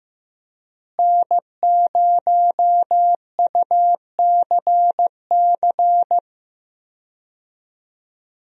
Morse code, heard as a keyed tone, about 15 wpm, 700 Hz. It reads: N0UCC
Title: USA Call Signs - 5 Characters - Repeated 2x - 15wpm